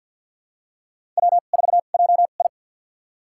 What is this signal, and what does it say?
Morse code, heard as a keyed tone, about 50 wpm, spaced as for 36 wpm, 700 Hz. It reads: W4JI
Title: USA Call Signs - 4 Characters - 2x Character Spacing - 50wpm